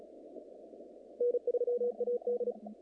Morse code, heard as a keyed tone, about 36 wpm, 480 Hz. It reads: N3AD
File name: contests / naqp-cw-k5tr-op-k5pi- -2200